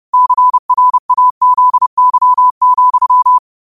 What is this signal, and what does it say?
Morse code, heard as a keyed tone, about 30 wpm, 1000 Hz. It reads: CRAZY,
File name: qotdfiles / 30wpm